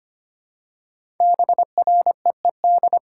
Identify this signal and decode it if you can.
Morse code, heard as a keyed tone, about 25 wpm, 700 Hz. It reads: BREED
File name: 5 Letter Words - First 3 Letters Given - 25wpm